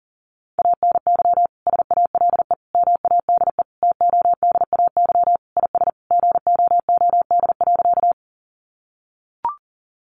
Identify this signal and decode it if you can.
Morse code, heard as a keyed tone, about 40 wpm, 700 Hz. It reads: ANY SALE MADE TODAY IS GOOD.
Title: Sentences from Top 1400 Words 40wpm